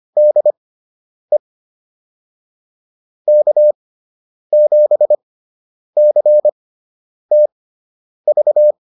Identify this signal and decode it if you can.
Morse code, heard as a keyed tone, about 25 wpm, spaced as for 9 wpm, 600 Hz. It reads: DE K7CTV